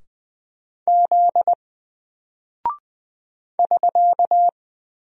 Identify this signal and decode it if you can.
Morse code, heard as a keyed tone, about 20 wpm, 700 Hz. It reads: Z <SK>
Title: Single Character (Letter:Number:Punctuation:Prosign) Alt 20wpm